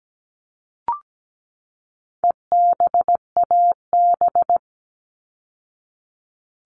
Morse code, heard as a keyed tone, about 17 wpm, 700 Hz. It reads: EBAB
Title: All Cut Numbers - 4-Digit 17wpm